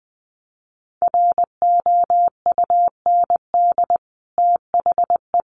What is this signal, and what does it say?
Morse code, heard as a keyed tone, about 20 wpm, 700 Hz. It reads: ROUND THE
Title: Sentences from Top 1000 Words 20wpm